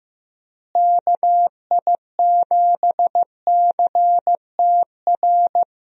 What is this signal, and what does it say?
Morse code, heard as a keyed tone, about 16 wpm, 700 Hz. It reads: KI7CTR
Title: USA Call Signs - 6 Characters - Repeated 2x - 15wpm